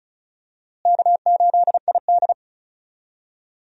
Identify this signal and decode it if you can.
Morse code, heard as a keyed tone, about 34 wpm, 700 Hz. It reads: K8ID
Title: USA Call Signs - 4 Characters - Repeated 2x - 35wpm